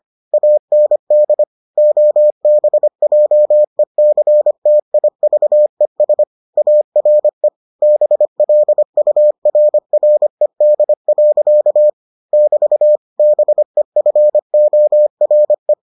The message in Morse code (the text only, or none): AND OBJECTIVES ARE BLURRED. <BT> BEFORE